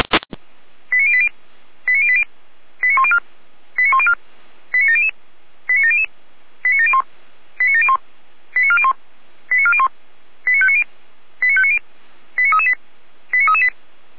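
Five-tone selective call (ZVEI), heard as quick runs of five tones, tones of about 70 ms. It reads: 89080 89080 89104 89104 89700 89700 89701 89701 89401 89401 89590 89590 89338 89338
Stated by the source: SDR-received ZVEI five-tone call 89280.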